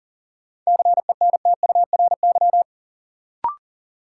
Morse code, heard as a keyed tone, about 40 wpm, 700 Hz. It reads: CENTURY
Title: Sets of 5 Words - New Word Drill - 40wpm